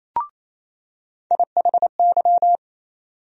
Morse code, heard as a keyed tone, about 28 wpm, 700 Hz. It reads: IHY
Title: USA Call Sign Postfixes - 3 Letters 28wpm